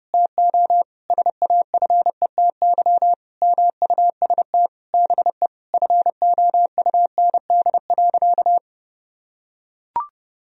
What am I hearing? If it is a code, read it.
Morse code, heard as a keyed tone, about 30 wpm, 700 Hz. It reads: TO SAFETY MUST BE FOUND.
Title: Sentences from Top 1000 Words - Review Part 4 - 30wpm